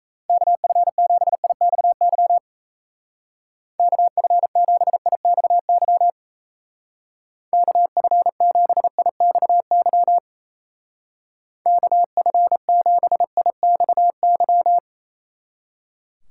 Morse code, heard as a keyed tone, about 34 wpm, 700 Hz. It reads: KF7IXY KF7IXY KF7IXY KF7IXY
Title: Speed Racing - USA Call Signs 28wpm